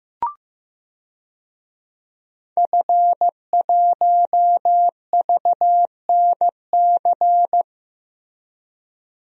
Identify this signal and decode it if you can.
Morse code, heard as a keyed tone, about 15 wpm, 700 Hz. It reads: F1VNC
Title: DXCC-100 Call Signs - 5 Characters - Repeated 3x - 15wpm